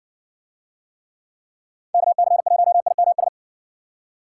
Morse code, heard as a keyed tone, about 60 wpm, 700 Hz. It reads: KC1IMS